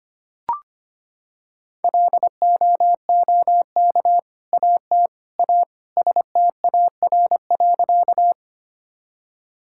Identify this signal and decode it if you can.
Morse code, heard as a keyed tone, about 25 wpm, 700 Hz. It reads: LOOK AT A STAR.
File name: practice / 25wpm / Sentences with 2-Word Phrases - 25wpm